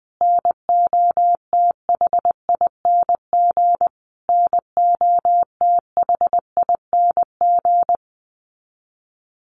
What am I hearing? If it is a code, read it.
Morse code, heard as a keyed tone, about 20 wpm, 700 Hz. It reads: NOTHING NOTHING
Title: Top 500 Words Repeated 20wpm